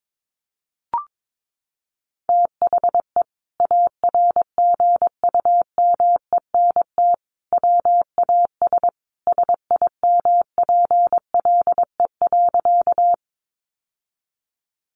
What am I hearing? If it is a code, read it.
Morse code, heard as a keyed tone, about 22 wpm, 700 Hz. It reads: THE ARGUMENT WAS SIMPLE.
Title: Sentences from Top 1400 Words 22wpm